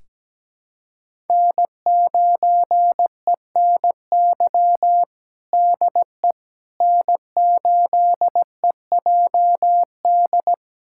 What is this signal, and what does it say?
Morse code, heard as a keyed tone, about 17 wpm, 700 Hz. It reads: N9ENY DE N8EJD
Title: USA Callsign DE Callsign - 5 Characters - 17wpm